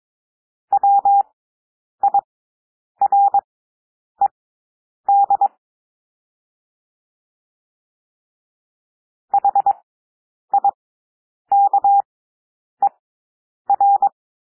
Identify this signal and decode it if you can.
Morse code, heard as a keyed tone, about 22 wpm, spaced as for 9 wpm, 800 Hz. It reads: WIRED HIKER
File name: cwmp3 / W2 SOUND WORDS 22 wpm